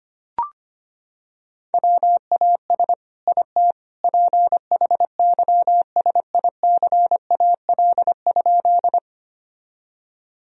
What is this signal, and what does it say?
Morse code, heard as a keyed tone, about 25 wpm, 700 Hz. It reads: WAS IT PHYSICAL?
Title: Sentences from Top 1000 Words - Review Part 4 - 25wpm